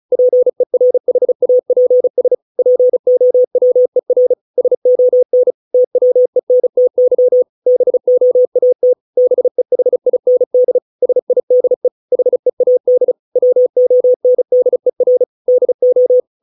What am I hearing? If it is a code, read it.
Morse code, heard as a keyed tone, about 34 wpm, 500 Hz. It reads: PERHAPS POWER SON TWENTY BOAT BEHIND SIDE HEAD WONDER DO